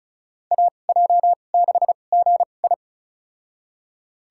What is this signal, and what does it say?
Morse code, heard as a keyed tone, about 36 wpm, spaced as for 25 wpm, 700 Hz. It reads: AJ6GI
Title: USA Call Signs - 5 Characters - 2x Character Spacing - 35wpm